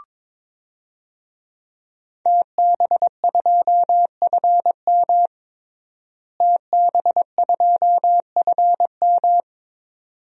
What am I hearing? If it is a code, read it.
Morse code, heard as a keyed tone, about 22 wpm, 700 Hz. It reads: TB2FM TB2FM
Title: DXCC-100 Call Signs - 5 Characters - Repeated 2x - 22wpm